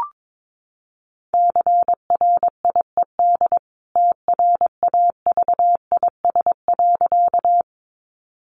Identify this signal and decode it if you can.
Morse code, heard as a keyed tone, about 22 wpm, 700 Hz. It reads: CRIED TRAVIS.
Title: A Sound Of Thunder 22wpm